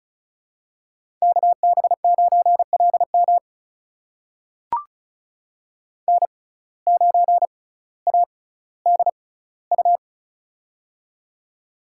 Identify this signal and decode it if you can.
Morse code, heard as a keyed tone, about 36 wpm, spaced as for 18 wpm, 700 Hz. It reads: KB9LM N9ADU
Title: USA Call Signs - 5 Characters - 6x Character Spacing - 35wpm